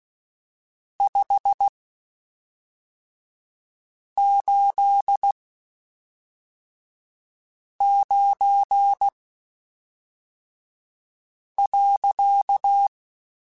Morse code, heard as a keyed tone, about 16 wpm, spaced as for 3 wpm, 800 Hz. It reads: 589.
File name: (spaced) / QSOs Spaced-mp3s / WD0FBT 1 @5 sp